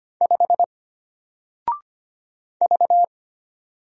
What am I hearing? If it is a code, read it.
Morse code, heard as a keyed tone, about 25 wpm, 700 Hz. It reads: 5 V
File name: Single-Letter-Number-25wpm